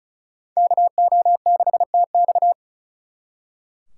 Morse code, heard as a keyed tone, about 47 wpm, spaced as for 42 wpm, 700 Hz. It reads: KO6TX
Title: Speed Racing - USA 2x2 Call Signs 35wpm